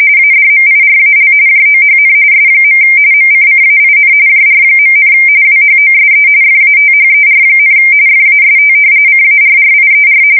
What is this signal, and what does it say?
RTTY, about 45.45 baud. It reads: ADMIRAL JOHN S. MCCAIN JR., WHOSE FATHER, FAST CARRIER
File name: fsk170 / battleshipnj